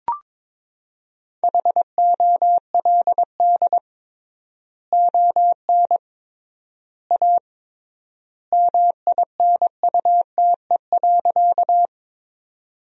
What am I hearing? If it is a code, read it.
Morse code, heard as a keyed tone, about 21 wpm, 700 Hz. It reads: HOLD ON A MINUTE.
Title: 3x Word-Spacing - Sentences from Top 500 Words - 22wpm